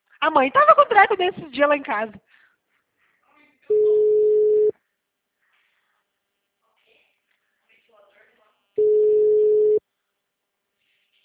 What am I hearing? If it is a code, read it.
Ringback tone (Europe, 425 Hz), with speech.